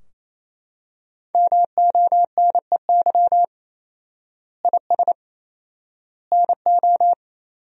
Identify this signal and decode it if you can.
Morse code, heard as a keyed tone, about 27 wpm, 700 Hz. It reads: MONEY IS NO
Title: 4x Word-Spacing - Sentences from Top 500 Words - 28wpm